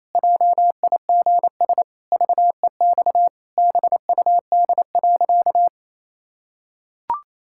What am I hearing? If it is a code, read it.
Morse code, heard as a keyed tone, about 27 wpm, 700 Hz. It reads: JIGS VEX BUD.
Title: Pangram Challenge v2 - 28wpm